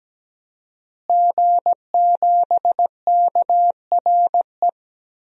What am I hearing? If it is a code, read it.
Morse code, heard as a keyed tone, about 17 wpm, 700 Hz. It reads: G7KRE
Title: DXCC-100 Call Signs - 5 Characters - Repeated 3x - 17wpm